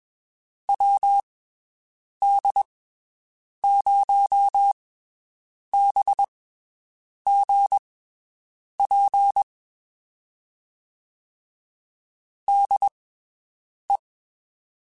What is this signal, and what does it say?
Morse code, heard as a keyed tone, about 22 wpm, spaced as for 7 wpm, 800 Hz. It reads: WD0BGP DE
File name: (spaced) / QSOs Spaced-mp3s / WD0BGP 1 @20 sp